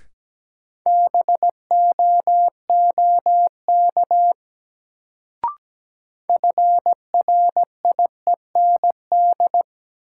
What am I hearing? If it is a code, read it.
Morse code, heard as a keyed tone, about 17 wpm, 700 Hz. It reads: BOOK FRIEND
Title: Sentences from Top 300 Words - New Word Drill - 17wpm